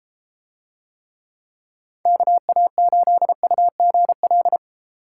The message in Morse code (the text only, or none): KA8UGL